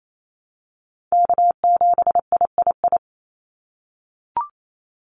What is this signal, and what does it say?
Morse code, heard as a keyed tone, about 28 wpm, 700 Hz. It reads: K7III
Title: NA SOTA Chaser Callsigns 28wpm